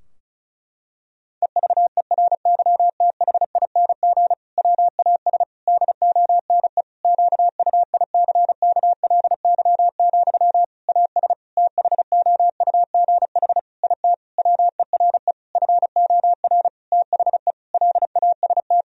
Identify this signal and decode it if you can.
Morse code, heard as a keyed tone, about 34 wpm, 700 Hz. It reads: EVERYTHING WAS DONE QUICKLY, AS THOUGH IT WERE FOR THE LAST
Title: Sentences from Top 1000 Words - Review Part 2 - 35wpm